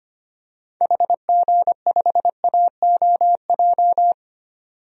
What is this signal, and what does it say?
Morse code, heard as a keyed tone, about 25 wpm, 690 Hz. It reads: HG5AOJ
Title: DXCC-100 Call Signs - 6 Characters - 25wpm